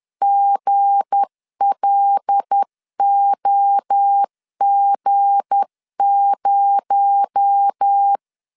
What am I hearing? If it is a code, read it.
Morse code, heard as a keyed tone, about 10 wpm, 800 Hz. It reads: GLOG0
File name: LEZIONI / 7 GF9Z / 7 G 10wpm 2501